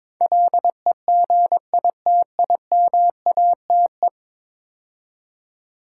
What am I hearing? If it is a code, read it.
Morse code, heard as a keyed tone, about 21 wpm, 700 Hz. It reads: LEGITIMATE
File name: practice / 22wpm / 10 Letter Words - First 3 Letters Given - 22wpm